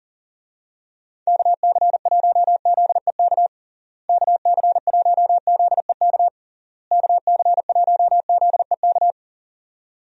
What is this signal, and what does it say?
Morse code, heard as a keyed tone, about 40 wpm, 700 Hz. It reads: KC1ZEK KC1ZEK KC1ZEK
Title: USA Call Signs - 6 Characters - Repeated 3x - 40wpm